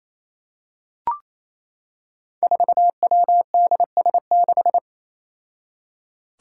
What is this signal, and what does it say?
Morse code, heard as a keyed tone, about 28 wpm, 700 Hz. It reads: 4WDS6
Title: Five Challenging Letter-Number Combinations 28wpm